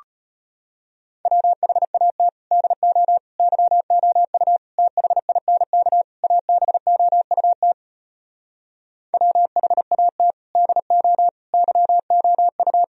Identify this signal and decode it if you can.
Morse code, heard as a keyed tone, about 33 wpm, 700 Hz. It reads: WHAT DO YOU THINK ABOUT WHAT DO YOU